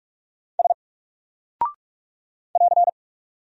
Morse code, heard as a keyed tone, about 44 wpm, 700 Hz. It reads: S <AR>
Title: Single Character (Letter:Number:Punctuation:Prosign) Alt 45wpm